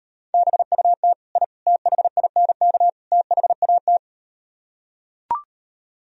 Morse code, heard as a keyed tone, about 39 wpm, 700 Hz. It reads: BUT I THINK THAT